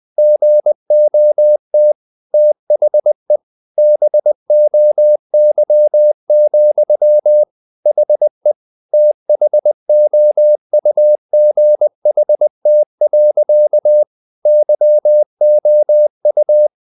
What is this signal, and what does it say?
Morse code, heard as a keyed tone, about 20 wpm, 600 Hz. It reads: GOT THE BOY, HE THOUGHT. YOU
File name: the old man and the sea / the old man and the sea part 1 20wpm